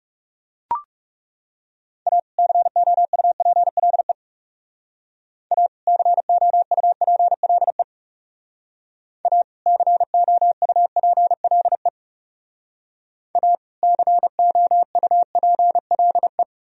Morse code, heard as a keyed tone, about 38 wpm, 700 Hz. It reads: A COUPLE A COUPLE A COUPLE A COUPLE